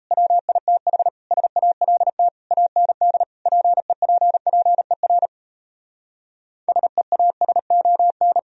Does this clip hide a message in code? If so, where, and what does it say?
Yes: Morse code from 0.1 to 8.4 s, heard as a keyed tone, about 38 wpm, 700 Hz. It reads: WITH SALT AND PEPPER SEASON